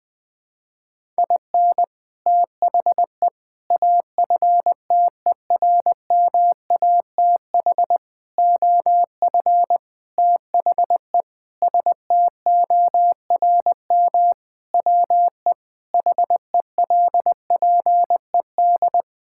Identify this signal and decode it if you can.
Morse code, heard as a keyed tone, about 21 wpm, 700 Hz. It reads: IN THE AFTERMATH OF THE STORM WE HELPED
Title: Sentences with 5-Word Phrases - 20wpm